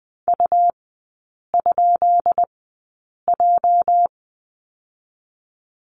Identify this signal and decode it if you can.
Morse code, heard as a keyed tone, about 20 wpm, spaced as for 8 wpm, 690 Hz. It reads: U?J
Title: 205 Lesson 33e - Letter j - Three Characters 20wpm